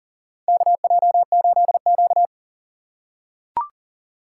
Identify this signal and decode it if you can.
Morse code, heard as a keyed tone, about 40 wpm, 700 Hz. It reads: KJ8Q
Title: USA Call Signs - 2x Character Spacing - 40wpm